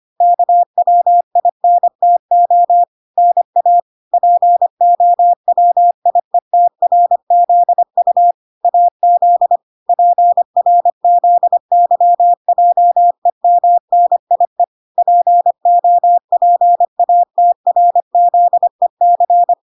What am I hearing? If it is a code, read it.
Morse code, heard as a keyed tone, about 25 wpm, 700 Hz. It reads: KWINTO NA POWIETRZU AZ PRZYJEMNIE POPATRZEC